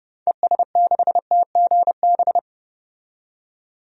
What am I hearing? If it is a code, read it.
Morse code, heard as a keyed tone, about 30 wpm, 700 Hz. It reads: ES6TGB
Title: DXCC-100 Call Signs - 6 Characters - 30wpm